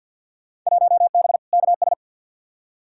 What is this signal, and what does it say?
Morse code, heard as a keyed tone, about 47 wpm, 700 Hz. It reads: 1B KS
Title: Field Day Exchanges Encore - 50wpm